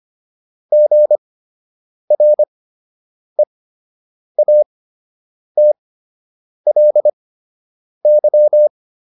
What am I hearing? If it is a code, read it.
Morse code, heard as a keyed tone, about 25 wpm, spaced as for 8 wpm, 600 Hz. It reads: GREATLY